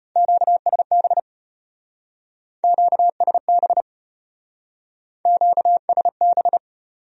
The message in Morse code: QSB QSB QSB

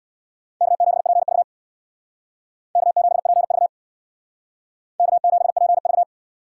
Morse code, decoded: K7PV K7PV K7PV

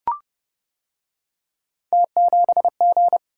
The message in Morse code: T7G

